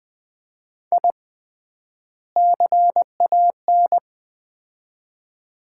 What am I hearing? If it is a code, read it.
Morse code, heard as a keyed tone, about 20 wpm, 700 Hz. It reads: I CAN
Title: 3x Word-Spacing - Sets of 2 Words - 20wpm